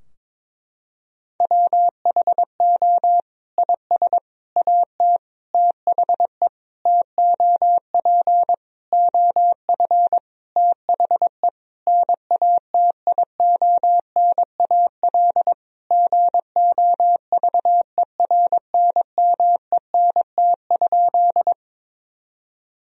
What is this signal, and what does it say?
Morse code, heard as a keyed tone, about 21 wpm, 700 Hz. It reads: WHO IS AT THE TOP OF THE NATIONAL GOVERNMENT?
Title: Sentences from Top 1000 Words - Review Part 4 - 22wpm